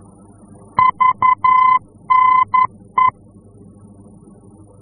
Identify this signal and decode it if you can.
Morse code, heard as a keyed tone, about 11 wpm, 1020 Hz. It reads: VNE
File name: france / mx / VNE-F-VOR